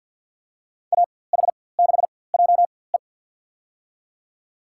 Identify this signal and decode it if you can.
Morse code, heard as a keyed tone, about 52 wpm, spaced as for 23 wpm, 700 Hz. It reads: AH6JE
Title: USA Call Signs - 5 Characters - 4x Character Spacing - 50wpm